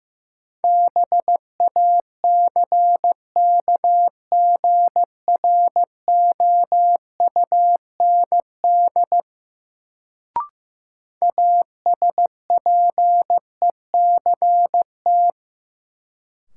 Morse code, heard as a keyed tone, about 15 wpm, 700 Hz. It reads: BACKGROUND ASPECT
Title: Sentences from Top 1300 Words - New Word Drill - 15wpm